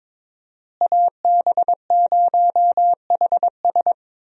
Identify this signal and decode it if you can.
Morse code, heard as a keyed tone, about 22 wpm, 700 Hz. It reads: AB0HS
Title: USA Call Signs - 5 Characters - Repeated 2x - 22wpm